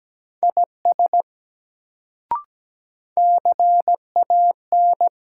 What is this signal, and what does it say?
Morse code, heard as a keyed tone, about 17 wpm, 700 Hz. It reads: IS CAN
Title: Sentences from Top 100 Words - New Word Drill - 17wpm